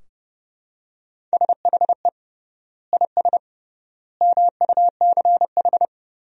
Morse code, heard as a keyed tone, about 30 wpm, 700 Hz. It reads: SHE IS MUCH
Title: 3x Word-Spacing - Sentences from Top 400 Words - 30wpm